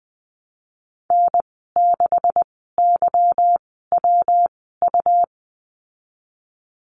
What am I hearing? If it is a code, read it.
Morse code, heard as a keyed tone, about 20 wpm, spaced as for 15 wpm, 700 Hz. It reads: N6YWU
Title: 161 Lesson 26g - Letter g - Call Signs 20wpm